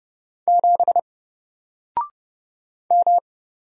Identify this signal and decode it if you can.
Morse code, heard as a keyed tone, about 31 wpm, 700 Hz. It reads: 7 M
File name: Single-Letter-Number-30wpm